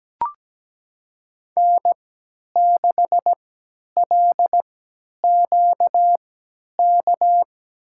Morse code, heard as a keyed tone, about 17 wpm, spaced as for 10 wpm, 700 Hz. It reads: N6LQK
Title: USA Call Signs - 5 Characters - 3x Character Spacing - 17wpm